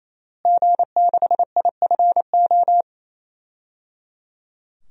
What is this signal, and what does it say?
Morse code, heard as a keyed tone, about 27 wpm, 700 Hz. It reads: G6IFO
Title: DXCC-100 Call Signs - 5 Characters - Repeated 3x - 28wpm